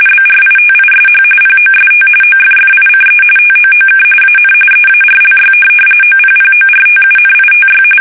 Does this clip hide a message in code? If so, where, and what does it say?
Yes: RTTY from 0.1 to 7.8 s, about 45.45 baud. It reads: BRITISH POST OFFICE TOOK AN EARLY OPPORTUNITY